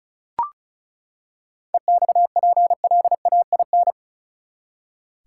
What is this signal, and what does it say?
Morse code, heard as a keyed tone, about 34 wpm, 700 Hz. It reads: EXPLAIN
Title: Seven Letter Words - 35wpm